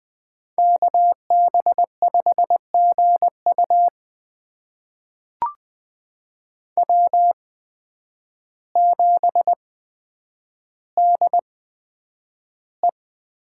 Morse code, heard as a keyed tone, about 20 wpm, spaced as for 9 wpm, 700 Hz. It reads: KB5GU W7DE
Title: USA Call Signs - 5 Characters - 8x Character Spacing - 20wpm